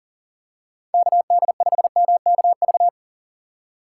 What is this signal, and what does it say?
Morse code, heard as a keyed tone, about 40 wpm, 700 Hz. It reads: KD5MKV
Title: USA Call Signs - 3x Character Spacing - 40wpm